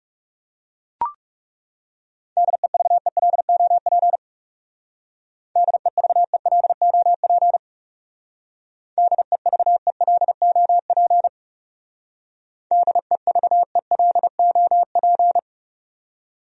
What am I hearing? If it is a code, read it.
Morse code, heard as a keyed tone, about 36 wpm, 700 Hz. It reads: DEVELOP DEVELOP DEVELOP DEVELOP
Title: Speed-Racing Top 500 Words 30wpm